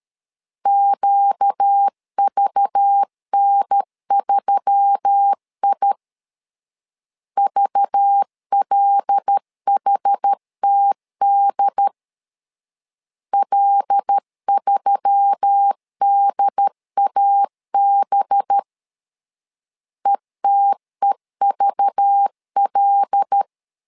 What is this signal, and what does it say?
Morse code, heard as a keyed tone, about 13 wpm, 800 Hz. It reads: QVN3I VLHTD L3DAB ETEVL
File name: LEZIONI / 5 LBVC / 5 LBV 12wpm 1855